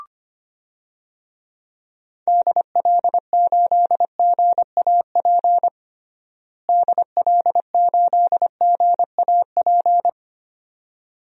Morse code, heard as a keyed tone, about 25 wpm, 700 Hz. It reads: DL8GAP DL8GAP
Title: DXCC-100 Call Signs - 6 Characters - Repeated 2x - 25wpm